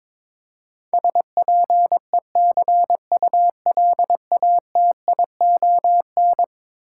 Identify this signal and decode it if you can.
Morse code, heard as a keyed tone, about 22 wpm, 700 Hz. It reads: SPECULATION
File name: practice / 22wpm / Eleven Letter Words - 22wpm